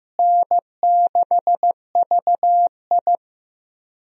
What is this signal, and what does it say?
Morse code, heard as a keyed tone, about 15 wpm, 700 Hz. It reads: N6VI
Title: USA Call Signs - 4 Characters - Repeated 2x - 15wpm